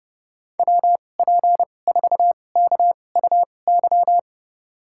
Morse code, heard as a keyed tone, about 30 wpm, spaced as for 22 wpm, 700 Hz. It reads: WP4KUY
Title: USA Call Signs - 2x Character Spacing - 30wpm